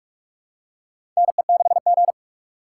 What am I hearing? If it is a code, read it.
Morse code, heard as a keyed tone, about 47 wpm, 700 Hz. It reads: NE6G
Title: USA Call Signs - 4 Characters - 4x Character Spacing - 45wpm